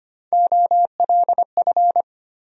Morse code, heard as a keyed tone, about 25 wpm, 700 Hz. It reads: OLF
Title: USA Call Sign Postfixes - 3 Letters 25wpm